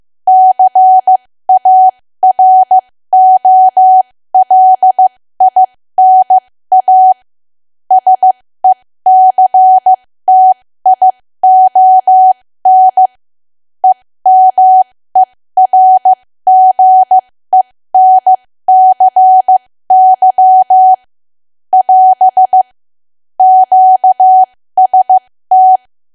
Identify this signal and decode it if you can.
Morse code, heard as a keyed tone, about 15 wpm, spaced as for 13 wpm, 750 Hz. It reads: CAROLINA SECTION EMERGENCY <AS> QST
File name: cwa / W1AW-Level2 / 13WPM